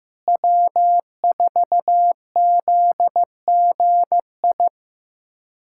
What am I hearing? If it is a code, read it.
Morse code, heard as a keyed tone, about 16 wpm, 700 Hz. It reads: W4ZGI
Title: USA Callsign DE Callsign 15wpm